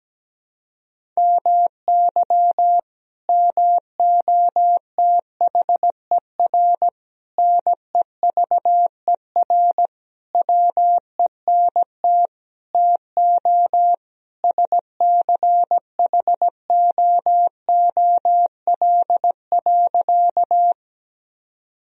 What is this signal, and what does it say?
Morse code, heard as a keyed tone, about 17 wpm, 700 Hz. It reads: MY MOTHER NEVER WENT TO SCHOOL.
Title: Sentences from Top 1000 Words - Review Part 1 - 17wpm